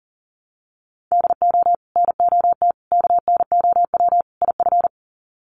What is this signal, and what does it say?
Morse code, heard as a keyed tone, about 40 wpm, 700 Hz. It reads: DO NOT KNOW IF
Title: Sets of 4 Words 40wpm